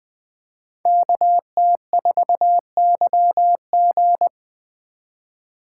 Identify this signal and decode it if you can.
Morse code, heard as a keyed tone, about 20 wpm, 700 Hz. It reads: KT4YG
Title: Speed Racing - USA 2x2 Call Signs 17wpm